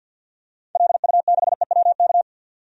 Morse code, heard as a keyed tone, about 48 wpm, 700 Hz. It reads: LU6EWK
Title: DXCC-100 Call Signs - 6 Characters - 50wpm